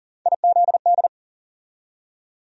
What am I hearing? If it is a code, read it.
Morse code, heard as a keyed tone, about 40 wpm, 700 Hz. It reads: IZD
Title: USA Call Sign Postfixes - 3 Letters 40wpm